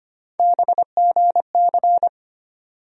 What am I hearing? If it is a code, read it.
Morse code, heard as a keyed tone, about 25 wpm, 700 Hz. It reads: BGC
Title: USA Call Sign Postfixes - 3 Letters 25wpm